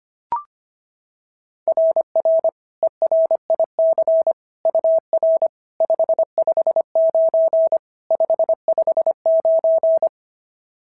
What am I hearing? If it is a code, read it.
Morse code, heard as a keyed tone, about 24 wpm, 650 Hz. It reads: RR ERIC UR 559 559
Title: SOTA QSO Practice 25wpm